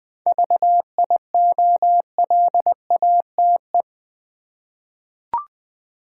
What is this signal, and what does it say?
Morse code, heard as a keyed tone, about 20 wpm, 700 Hz. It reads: VIOLATE
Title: 7 Letter Words - First 3 Letters Given - 20wpm